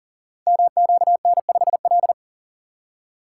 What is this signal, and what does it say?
Morse code, heard as a keyed tone, about 38 wpm, 700 Hz. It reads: MQN5L
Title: Code Groups - 5 Letter:Numbers - 40wpm